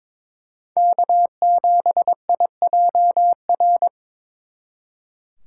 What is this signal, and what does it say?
Morse code, heard as a keyed tone, about 26 wpm, 700 Hz. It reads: K7IJR
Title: Speed Racing - USA 1x3 Call Signs 22wpm